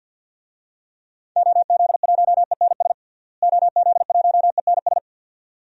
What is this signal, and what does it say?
Morse code, heard as a keyed tone, about 51 wpm, 700 Hz. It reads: OZ1ENS OZ1ENS
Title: DXCC-100 Call Signs - 6 Characters - Repeated 2x - 50wpm